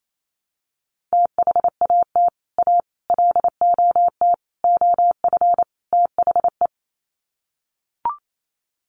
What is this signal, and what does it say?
Morse code, heard as a keyed tone, about 28 wpm, 700 Hz. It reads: THAT A LOT OF THE
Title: Sets of 5 Words Encore2 28wpm